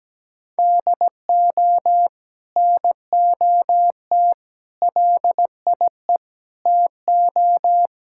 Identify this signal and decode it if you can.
Morse code, heard as a keyed tone, about 17 wpm, 700 Hz. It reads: DO NOT LIE TO
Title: Sentences from Top 1600 Words 17wpm